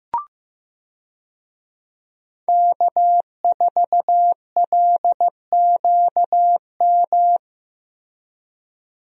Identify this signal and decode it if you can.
Morse code, heard as a keyed tone, about 15 wpm, 700 Hz. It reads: K4LQM
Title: USA Call Signs - 5 Characters - Repeated 3x - 15wpm